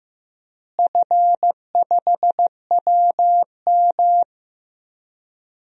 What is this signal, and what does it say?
Morse code, heard as a keyed tone, about 15 wpm, 700 Hz. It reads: F5WM
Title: DXCC-100 Call Signs - 4 Characters - Repeated 3x - 15wpm